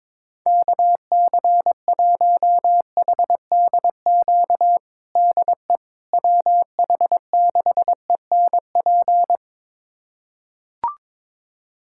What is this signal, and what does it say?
Morse code, heard as a keyed tone, about 22 wpm, 700 Hz. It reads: KC1HDQ DE WH6ENP
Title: USA Callsign DE Callsign - 6 Characters - 22wpm